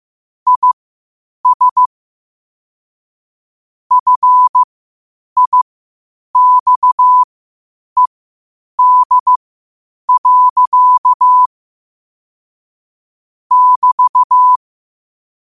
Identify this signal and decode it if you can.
Morse code, heard as a keyed tone, about 15 wpm, spaced as for 8 wpm, 1000 Hz. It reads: IS FIXED. <BT>
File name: qotdfiles / 8wpm